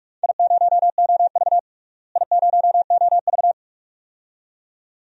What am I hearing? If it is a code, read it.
Morse code, heard as a keyed tone, about 42 wpm, 700 Hz. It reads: I0OV I0OV
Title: DXCC-100 Call Signs - 4 Characters - Repeated 2x - 45wpm